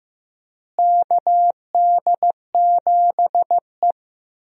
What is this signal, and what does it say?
Morse code, heard as a keyed tone, about 15 wpm, 700 Hz. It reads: KD7E
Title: USA Call Signs - 4 Characters - Repeated 2x - 15wpm